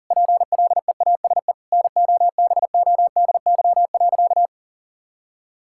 Morse code, heard as a keyed tone, about 40 wpm, 700 Hz. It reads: PLEASE NOBODY.
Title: Sentences from Top 1600 Words 40wpm